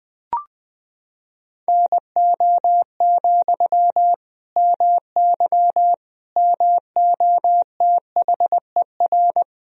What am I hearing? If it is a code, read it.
Morse code, heard as a keyed tone, about 21 wpm, 700 Hz. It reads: NO, MY MOTHER
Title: Sentences from Top 1000 Words - Review Part 2 - 20wpm